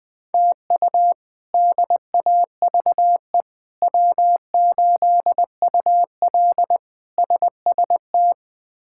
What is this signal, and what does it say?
Morse code, heard as a keyed tone, about 20 wpm, 700 Hz. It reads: TU DAVE W8UL SST
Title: K1USN SST QSO Practice - 20wpm